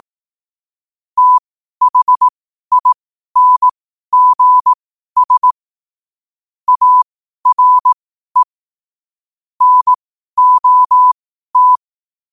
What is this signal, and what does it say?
Morse code, heard as a keyed tone, about 18 wpm, spaced as for 12 wpm, 1000 Hz. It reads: THINGS ARE NOT